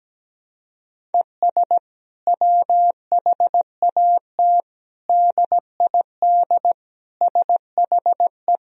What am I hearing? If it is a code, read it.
Morse code, heard as a keyed tone, about 17 wpm, 700 Hz. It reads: ES WHAT DID SHE
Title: Sentences from Top 200 Words and Abbreviations 17wpm